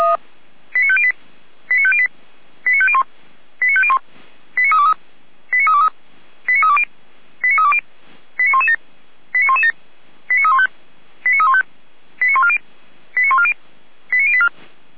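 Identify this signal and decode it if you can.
Five-tone selective call (ZVEI), heard as quick runs of five tones, tones of about 70 ms. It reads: 89508 89508 89501 89501 89323 89323 89320 89320 89117 89117 89315 89315 89140 89140 89084
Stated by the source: Received by SDR as a ZVEI five-tone selective call.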